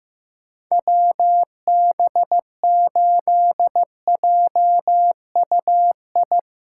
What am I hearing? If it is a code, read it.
Morse code, heard as a keyed tone, about 15 wpm, 700 Hz. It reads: WB8JUI
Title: K1USN SST QSO Practice - 15wpm